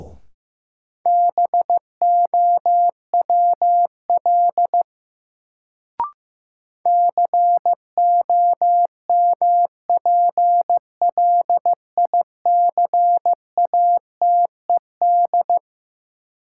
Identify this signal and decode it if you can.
Morse code, heard as a keyed tone, about 15 wpm, 700 Hz. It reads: BOWL COMPLICATED
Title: Sentences from Top 1200 Words - New Word Drill - 15wpm